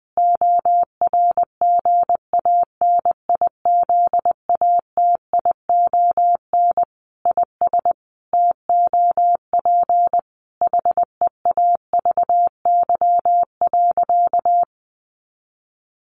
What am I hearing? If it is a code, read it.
Morse code, heard as a keyed tone, about 20 wpm, 700 Hz. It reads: ORGANIZATION IS TOP HEAVY.
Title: Sentences from Top 1100 Words 20wpm